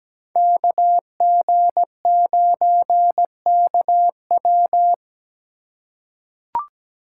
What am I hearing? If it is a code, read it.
Morse code, heard as a keyed tone, about 17 wpm, 700 Hz. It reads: KG9KW
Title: USA Call Signs - 5 Characters - 3x Character Spacing - 17wpm